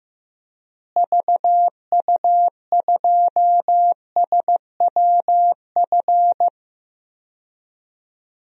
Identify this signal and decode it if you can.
Morse code, heard as a keyed tone, about 16 wpm, 700 Hz. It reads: VU2SWF
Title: DXCC-100 Call Signs - 6 Characters - Repeated 2x - 15wpm